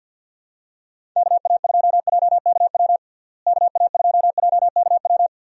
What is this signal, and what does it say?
Morse code, heard as a keyed tone, about 52 wpm, 700 Hz. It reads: KA2JKW KA2JKW
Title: USA Call Signs - 6 Characters - Repeated 3x - 50wpm